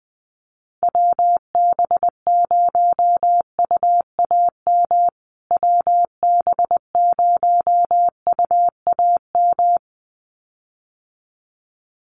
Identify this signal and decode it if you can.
Morse code, heard as a keyed tone, about 20 wpm, 700 Hz. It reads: WB0UAM WB0UAM
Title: USA Call Signs Repeated 2x 20wpm